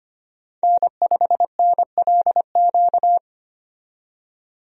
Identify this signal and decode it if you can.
Morse code, heard as a keyed tone, about 25 wpm, 700 Hz. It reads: N5NLQ
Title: USA Call Signs - 5 Characters - 25wpm